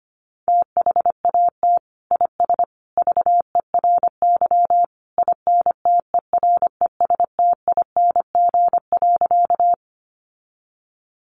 Morse code, heard as a keyed tone, about 25 wpm, 700 Hz. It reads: THAT IS VERY INTERESTING.
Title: Sentences from Top 1000 Words 25wpm